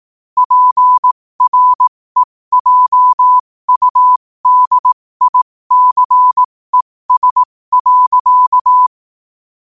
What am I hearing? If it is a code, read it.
Morse code, heard as a keyed tone, about 18 wpm, spaced as for 15 wpm, 1000 Hz. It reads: PREJUDICES.
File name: qotdfiles / 15wpm